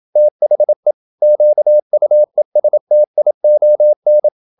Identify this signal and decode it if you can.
Morse code, heard as a keyed tone, about 27 wpm, 600 Hz. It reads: THE QUESTION
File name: Never 0027wpm